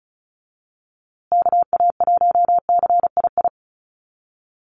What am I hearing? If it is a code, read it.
Morse code, heard as a keyed tone, about 34 wpm, 700 Hz. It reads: KA1CII
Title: USA Call Signs Repeated 3x 35wpm